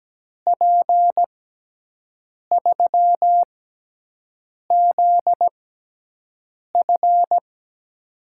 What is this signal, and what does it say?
Morse code, heard as a keyed tone, about 17 wpm, spaced as for 6 wpm, 700 Hz. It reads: P3ZF
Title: USA Call Signs - 5 Characters - 6x Character Spacing - 17wpm